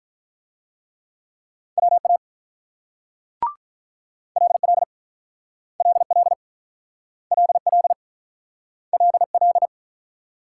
Morse code, heard as a keyed tone, about 44 wpm, 700 Hz. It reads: WA LL LL LL LL